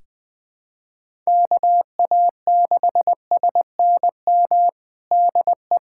Morse code, heard as a keyed tone, about 20 wpm, 700 Hz. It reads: KA6SNM DE
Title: USA Callsign DE Callsign - 6 Characters - 20wpm